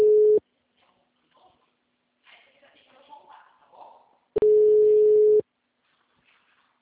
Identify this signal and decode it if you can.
Ringback tone (Europe, 425 Hz), with speech.
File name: out- -2020- - - .17860